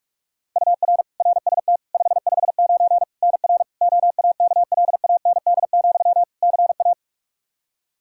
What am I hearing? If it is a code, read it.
Morse code, heard as a keyed tone, about 42 wpm, 700 Hz. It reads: UR RST 559 NR OAKLAND, CA
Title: K1USN SST QSO Practice 45wpm